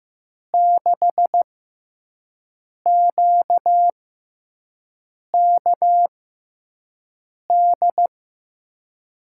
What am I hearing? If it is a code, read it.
Morse code, heard as a keyed tone, about 15 wpm, spaced as for 5 wpm, 700 Hz. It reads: 6QKD
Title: USA Call Signs - 6 Characters - 6x Character Spacing - 15wpm